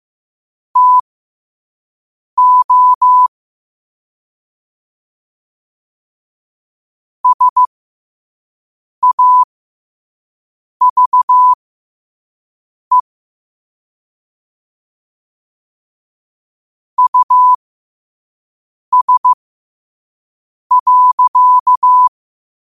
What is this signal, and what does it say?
Morse code, heard as a keyed tone, about 15 wpm, spaced as for 5 wpm, 1000 Hz. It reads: TO SAVE US.
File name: qotdfiles / 5wpm